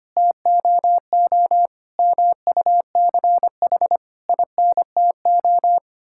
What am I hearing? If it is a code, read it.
Morse code, heard as a keyed tone, about 25 wpm, 700 Hz. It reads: TOO MUCH INTO
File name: Sets of 9 Words - 25wpm